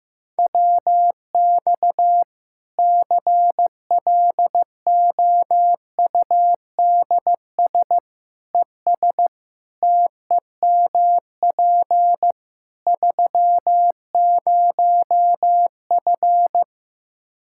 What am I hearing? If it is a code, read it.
Morse code, heard as a keyed tone, about 16 wpm, 700 Hz. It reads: WX CLOUDS ES TEMP 30F
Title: K1USN SST QSO Practice 15wpm